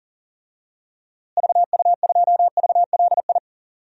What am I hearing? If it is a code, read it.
Morse code, heard as a keyed tone, about 40 wpm, 700 Hz. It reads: VU2VLI
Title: DXCC-100 Call Signs - 6 Characters - Repeated 2x - 40wpm